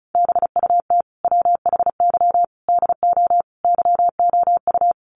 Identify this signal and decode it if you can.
Morse code, heard as a keyed tone, about 34 wpm, 700 Hz. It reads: BUT WHY DO YOU
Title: Sentences from Top 200 Words 35wpm